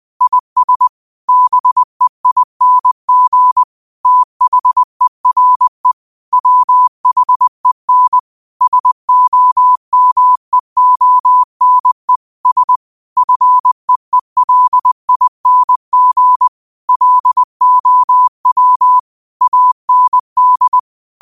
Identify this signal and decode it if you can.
Morse code, heard as a keyed tone, about 20 wpm, 1000 Hz. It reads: IS BEING THERE WHEN SOMEONE S FEELING LOW AND
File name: qotdfiles / 20wpm